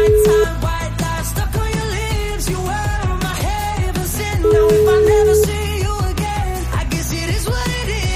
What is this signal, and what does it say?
Ringback tone (Europe, 425 Hz), with noise.